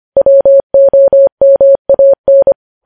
Morse code, heard as a keyed tone, about 25 wpm, 550 Hz. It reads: WOMAN
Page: News Headlines in Morse Code at 25 WPM